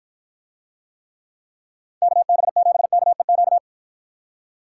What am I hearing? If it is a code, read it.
Morse code, heard as a keyed tone, about 53 wpm, 700 Hz. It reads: KB7KEQ